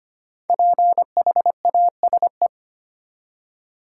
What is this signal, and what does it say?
Morse code, heard as a keyed tone, about 25 wpm, 700 Hz. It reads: PHASE
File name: 5 Letter Words - First 3 Letters Given - 25wpm